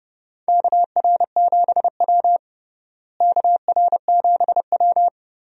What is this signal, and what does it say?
Morse code, heard as a keyed tone, about 30 wpm, 700 Hz. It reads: KR7W KR7W
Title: USA Call Signs - 4 Characters - Repeated 3x - 30wpm